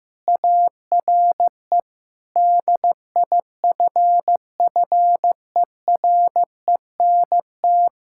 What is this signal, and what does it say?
Morse code, heard as a keyed tone, about 15 wpm, 700 Hz. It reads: ARE DIFFERENT
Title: Sentences from Top 1000 Words - Review Part 6 - 15wpm